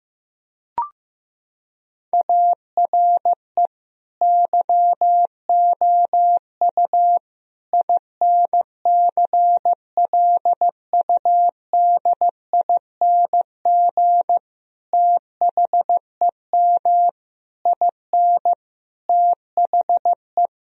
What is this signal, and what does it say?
Morse code, heard as a keyed tone, about 15 wpm, 700 Hz. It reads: ARE YOU INCLUDING THEM IN THE
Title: Sentences from Top 1000 Words - Review Part 4 - 15wpm